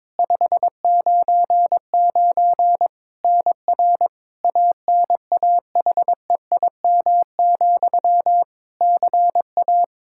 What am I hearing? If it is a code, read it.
Morse code, heard as a keyed tone, about 22 wpm, 700 Hz. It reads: 599 NR ANAHEIM, CA
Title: Practice QSO - Exchange 1 - Beginning - 22wpm